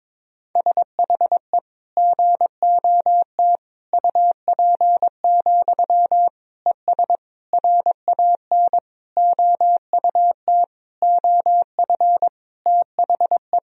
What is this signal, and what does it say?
Morse code, heard as a keyed tone, about 22 wpm, 700 Hz. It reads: SHE GOT UP, ES RAN OUT OF THE
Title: Sentences from Top 500 Words and Abbreviations 22wpm